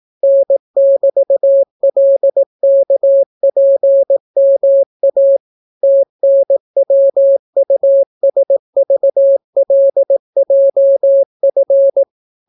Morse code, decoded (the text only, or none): N<BT>LKPMA TNWUSVLJF